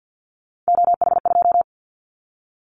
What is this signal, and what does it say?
Morse code, heard as a keyed tone, about 50 wpm, 700 Hz. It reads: OH2
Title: Callsign Prefixes for Top 50 DXCC - 50wpm